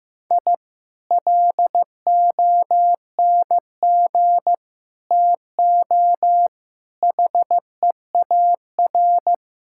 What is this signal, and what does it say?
Morse code, heard as a keyed tone, about 15 wpm, 700 Hz. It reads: I LONG TO HEAR
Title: Sentences from Top 1000 Words - Review Part 5 - 15wpm